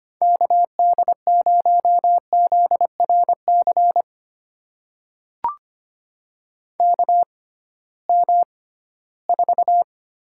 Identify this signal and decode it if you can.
Morse code, heard as a keyed tone, about 25 wpm, spaced as for 16 wpm, 700 Hz. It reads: KD0ZRC KM4
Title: USA Call Signs - 6 Characters - 6x Character Spacing - 25wpm